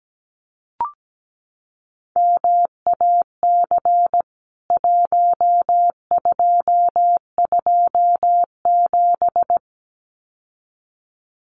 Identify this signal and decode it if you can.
Morse code, heard as a keyed tone, about 17 wpm, 700 Hz. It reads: MAC 1227
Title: CWT Exchange 17wpm